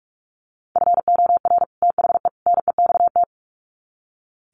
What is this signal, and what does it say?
Morse code, heard as a keyed tone, about 44 wpm, 700 Hz. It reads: FOR THE NEXT
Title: Sets of 3 Words 45wpm